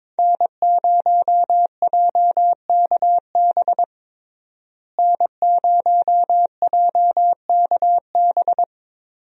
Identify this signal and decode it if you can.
Morse code, heard as a keyed tone, about 22 wpm, 700 Hz. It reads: N0JKB N0JKB
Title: USA Call Signs - 5 Characters - Repeated 2x - 22wpm